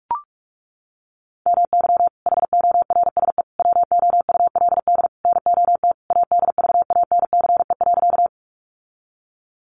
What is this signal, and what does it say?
Morse code, heard as a keyed tone, about 44 wpm, 700 Hz. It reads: MY HORSE WOULD NOT ADVANCE.
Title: Sentences from Top 1400 Words 45wpm